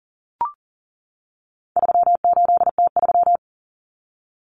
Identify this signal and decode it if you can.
Morse code, heard as a keyed tone, about 40 wpm, 700 Hz. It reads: 38T3
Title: Common Cut Numbers - 4-Digit 40wpm